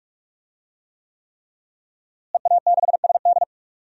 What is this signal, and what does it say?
Morse code, heard as a keyed tone, about 46 wpm, 700 Hz. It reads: EA6SD
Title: DXCC-100 Call Signs - 5 Characters - Repeated 3x - 45wpm